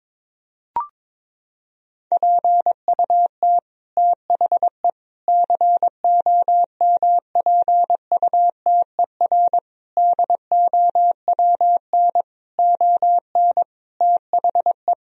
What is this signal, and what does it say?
Morse code, heard as a keyed tone, about 22 wpm, 700 Hz. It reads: PUT THE COMPUTER DOWN ON THE
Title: Sentences from Top 1000 Words - Review Part 3 - 22wpm